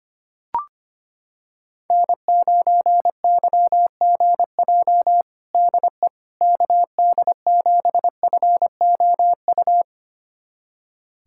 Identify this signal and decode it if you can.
Morse code, heard as a keyed tone, about 25 wpm, 700 Hz. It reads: N9YGJ DE KD7FOU
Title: USA Callsign DE Callsign 25wpm